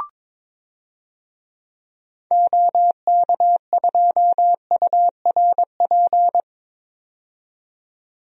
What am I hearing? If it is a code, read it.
Morse code, heard as a keyed tone, about 22 wpm, 700 Hz. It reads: OK2URP
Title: DXCC-100 Call Signs - 6 Characters - 22wpm